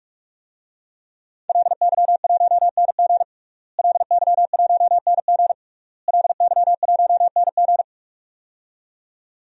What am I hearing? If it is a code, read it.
Morse code, heard as a keyed tone, about 46 wpm, 700 Hz. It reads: LY1NG LY1NG LY1NG
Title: DXCC-100 Call Signs - 5 Characters - Repeated 3x - 45wpm